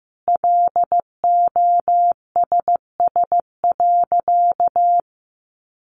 Morse code, heard as a keyed tone, about 14 wpm, 700 Hz. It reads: LOSS.
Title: Sentences from Top 1400 Words 15wpm